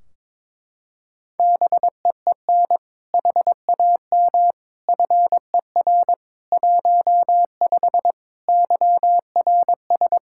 Morse code, heard as a keyed tone, about 22 wpm, 700 Hz. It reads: BEEN HAM FER 15 YRS
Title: Practice QSO - Exchange 3 - More Info - 22wpm